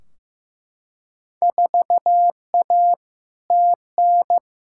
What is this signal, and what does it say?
Morse code, heard as a keyed tone, about 15 wpm, 700 Hz. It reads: 4A TN
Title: Field Day Exchanges - 15wpm